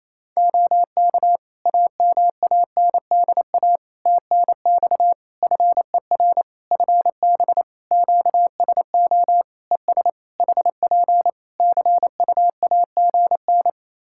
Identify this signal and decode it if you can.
Morse code, heard as a keyed tone, about 27 wpm, 700 Hz. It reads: OK AMANDA TNX FER FB QSO ES HP CUAGN
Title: Practice QSO - Exchange 4 - Ending - 28wpm